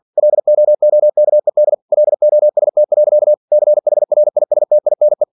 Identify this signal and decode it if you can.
Morse code, heard as a keyed tone, about 49 wpm, 600 Hz. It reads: LOOKED LOST. CHRISTINE
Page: Never 47wpm